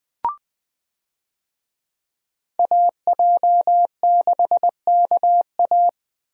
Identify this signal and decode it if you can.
Morse code, heard as a keyed tone, about 20 wpm, 700 Hz. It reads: AJ6KA
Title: USA Call Signs - 5 Characters - Repeated 2x - 20wpm